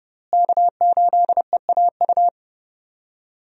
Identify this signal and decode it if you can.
Morse code, heard as a keyed tone, about 30 wpm, 700 Hz. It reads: K8EAU
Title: USA Call Signs - 5 Characters - Repeated 2x - 30wpm